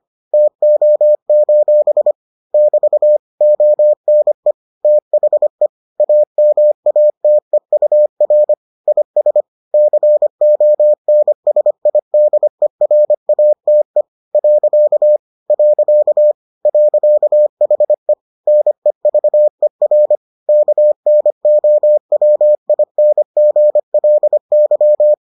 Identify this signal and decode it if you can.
Morse code, heard as a keyed tone, about 25 wpm, 600 Hz. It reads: TO: <BT> ONE THE AMATEUR IS CONSIDERATE . .HE NEVER KNOWINGLY